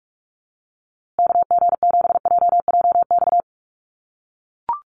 Morse code, heard as a keyed tone, about 44 wpm, 700 Hz. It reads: KG7JJX
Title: USA Call Signs Repeated 2x 45wpm